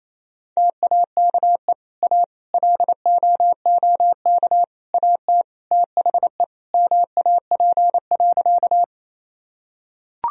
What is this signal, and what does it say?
Morse code, heard as a keyed tone, about 28 wpm, 700 Hz. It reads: TAKE A LOOK AT THE MAP.
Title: Sentences from Top 1000 Words - Review Part 2 - 28wpm